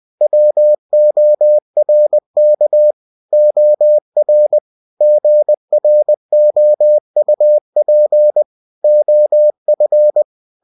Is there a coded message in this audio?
Yes: Morse code from 0.2 to 10.2 s, heard as a keyed tone, about 20 wpm, 600 Hz. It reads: WORK OR GROUP OF